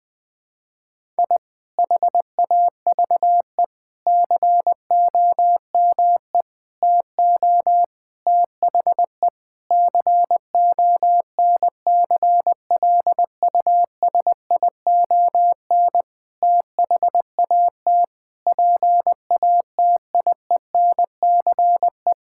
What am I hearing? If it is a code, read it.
Morse code, heard as a keyed tone, about 20 wpm, 700 Hz. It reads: I HAVE COME TO THE CONCLUSION THAT PATIENCE
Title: Sentences with 7-Word Phrases - 20wpm